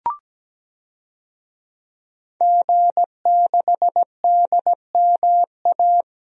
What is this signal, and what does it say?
Morse code, heard as a keyed tone, about 17 wpm, 700 Hz. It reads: G6DMA
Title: DXCC-100 Call Signs - 5 Characters - Repeated 3x - 17wpm